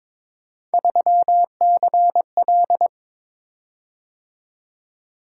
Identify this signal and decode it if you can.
Morse code, heard as a keyed tone, about 22 wpm, 700 Hz. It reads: 3CL
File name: practice / 22wpm / Three Letter-Number Combinations - 22wpm